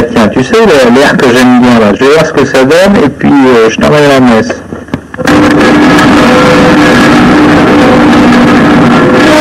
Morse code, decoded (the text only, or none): none